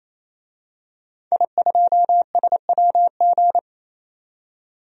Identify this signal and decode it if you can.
Morse code, heard as a keyed tone, about 28 wpm, 700 Hz. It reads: I2SWG
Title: DXCC-100 Call Signs - 5 Characters - 28wpm